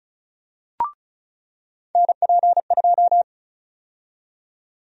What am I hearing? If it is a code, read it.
Morse code, heard as a keyed tone, about 36 wpm, 700 Hz. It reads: NP2
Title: USA Call Sign Prefixes - 35wpm